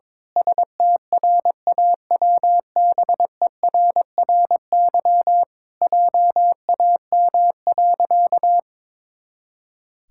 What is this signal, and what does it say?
Morse code, heard as a keyed tone, about 22 wpm, 700 Hz. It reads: STRAWBERRY JAM.
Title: Pangram Challenge v2 - 22wpm